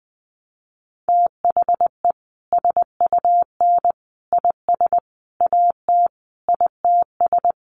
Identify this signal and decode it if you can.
Morse code, heard as a keyed tone, about 20 wpm, 700 Hz. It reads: THE SUN IS AT ITS
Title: Sentences from Top 1300 Words 20wpm